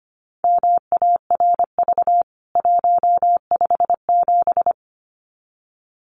Morse code, heard as a keyed tone, about 25 wpm, 700 Hz. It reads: MARV 157
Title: CWT Exchange 25wpm